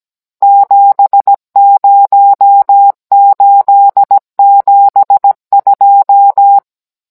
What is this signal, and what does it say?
Morse code, heard as a keyed tone, about 17 wpm, 800 Hz. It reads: 70872
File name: Cours-NR18-16mm